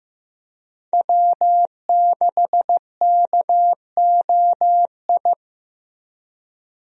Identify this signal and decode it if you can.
Morse code, heard as a keyed tone, about 15 wpm, 700 Hz. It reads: W6KOI
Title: USA Call Signs - 5 Characters - Repeated 3x - 15wpm